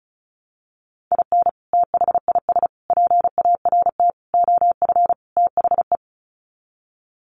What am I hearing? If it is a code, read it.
Morse code, heard as a keyed tone, about 34 wpm, 700 Hz. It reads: IN THIS PART OF THE
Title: Sets of 5 Words 35wpm